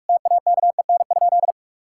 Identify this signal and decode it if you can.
Morse code, heard as a keyed tone, about 48 wpm, 700 Hz. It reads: TAKEN?
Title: Sentences from Top 1000 Words - Review Part 6 - 45wpm